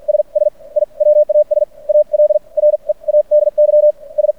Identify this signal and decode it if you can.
Morse code, heard as a keyed tone, about 50 wpm, spaced as for 39 wpm, 600 Hz. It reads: SS I WAS ALREADY S
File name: archived / CRSnet